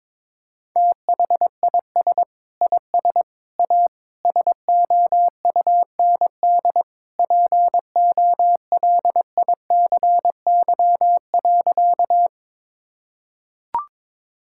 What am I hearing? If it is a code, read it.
Morse code, heard as a keyed tone, about 22 wpm, 700 Hz. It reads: THIS IS A SOUND POLICY.
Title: Sentences from Top 1000 Words - Review Part 5 - 22wpm